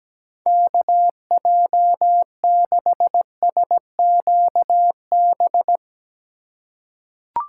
Morse code, heard as a keyed tone, about 17 wpm, 700 Hz. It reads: KJ6SQB
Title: USA Call Signs - 6 Characters - 3x Character Spacing - 17wpm